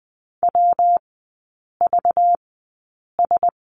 Morse code, heard as a keyed tone, about 20 wpm, spaced as for 8 wpm, 700 Hz. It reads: WVS
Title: 191 Lesson 31e - Letter v - Three Characters 20wpm